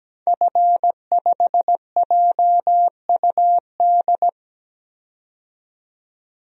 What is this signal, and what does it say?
Morse code, heard as a keyed tone, about 17 wpm, 700 Hz. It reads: F5JUD
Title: DXCC-100 Call Signs - 5 Characters - 17wpm